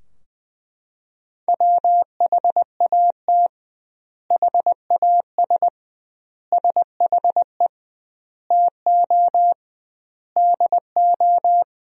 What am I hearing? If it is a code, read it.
Morse code, heard as a keyed tone, about 20 wpm, 700 Hz. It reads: WHAT HAS SHE TO DO
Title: 2x Word-Spacing - Sentences from Top 100 Words - 20wpm